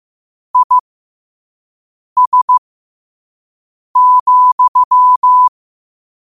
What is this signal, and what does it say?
Morse code, heard as a keyed tone, about 15 wpm, spaced as for 5 wpm, 1000 Hz. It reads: IS,
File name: qotdfiles / 5wpm